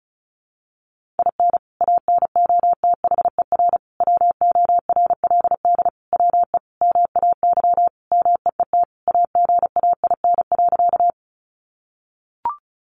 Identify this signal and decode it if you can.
Morse code, heard as a keyed tone, about 34 wpm, 700 Hz. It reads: IN ANOTHER WORLD WE MAY MEET AGAIN.
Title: Sentences from Top 1100 Words 35wpm